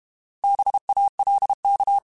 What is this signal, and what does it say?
Morse code, heard as a keyed tone, about 32 wpm, 800 Hz. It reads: BALK
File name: Alphabetical / Alphabetical mp3s / Bs @30